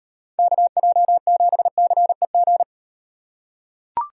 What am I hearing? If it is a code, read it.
Morse code, heard as a keyed tone, about 38 wpm, 700 Hz. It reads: KJ7CEG